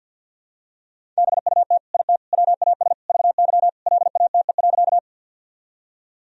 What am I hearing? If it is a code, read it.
Morse code, heard as a keyed tone, about 48 wpm, 700 Hz. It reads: BUT IT WAS VY LATE.
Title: Sentences from Top 300 Words and Abbreviations 50wpm